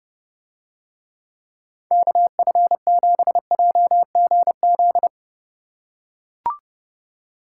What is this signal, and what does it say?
Morse code, heard as a keyed tone, about 30 wpm, 700 Hz. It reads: KF7JGZ